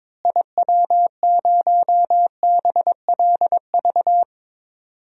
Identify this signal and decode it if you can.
Morse code, heard as a keyed tone, about 22 wpm, 700 Hz. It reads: IW0BLV
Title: DXCC-100 Call Signs - 6 Characters - Repeated 2x - 22wpm